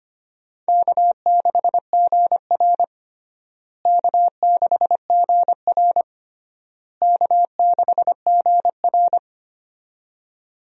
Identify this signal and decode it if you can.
Morse code, heard as a keyed tone, about 25 wpm, 700 Hz. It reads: K6GR K6GR K6GR
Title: USA Call Signs - 4 Characters - Repeated 3x - 25wpm